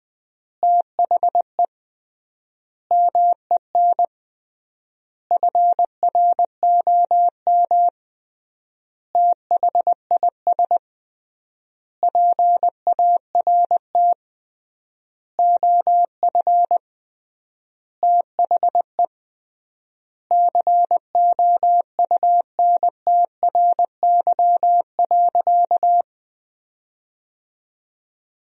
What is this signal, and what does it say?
Morse code, heard as a keyed tone, about 20 wpm, 700 Hz. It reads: THE MEN FROM THIS PART OF THE COUNTRY.
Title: 3x Word-Spacing - Sentences from Top 300 Words - 20wpm